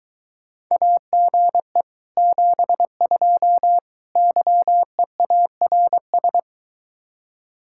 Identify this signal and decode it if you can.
Morse code, heard as a keyed tone, about 23 wpm, 700 Hz. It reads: AGE 72 YEARS